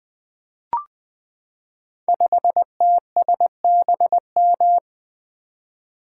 Morse code, heard as a keyed tone, about 20 wpm, 700 Hz. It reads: 5TSBM
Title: Code Groups - 5 Letter:Numbers - 20wpm